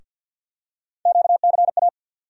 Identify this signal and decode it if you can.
Morse code, heard as a keyed tone, about 53 wpm, 700 Hz. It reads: QCA
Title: USA Call Sign Postfixes - 3 Letters 50wpm